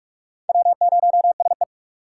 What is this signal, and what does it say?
Morse code, heard as a keyed tone, about 43 wpm, 700 Hz. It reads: W0SE
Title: USA Call Signs - 4 Characters - 6x Character Spacing - 45wpm